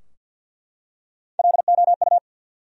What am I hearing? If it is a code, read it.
Morse code, heard as a keyed tone, about 52 wpm, 700 Hz. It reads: LOU
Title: USA Common Names - Max 3 Letters - 50wpm